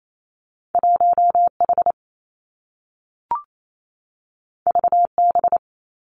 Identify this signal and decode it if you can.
Morse code, heard as a keyed tone, about 28 wpm, 700 Hz. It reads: 1 H V B
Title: Two Challenging Letter-Number Combinations 28wpm